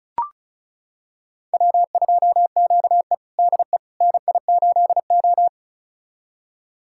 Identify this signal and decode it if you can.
Morse code, heard as a keyed tone, about 34 wpm, 700 Hz. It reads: W2QE DE NI8O
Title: USA Callsign DE Callsign - 4 Characters - 35wpm